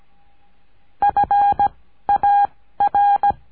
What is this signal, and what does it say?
Morse code, heard as a keyed tone, about 17 wpm, 810 Hz. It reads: F A R
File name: cursos / CURSO DE TELEGRAFIA / Telegrafia - Trilha 14